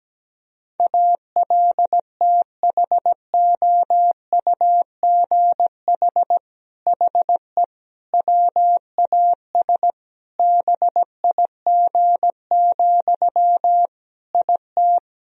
Morse code, heard as a keyed tone, about 17 wpm, 700 Hz. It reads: ALTHOUGH HE WAS BIG, IT
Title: Sentences from Top 1000 Words - Review Part 3 - 17wpm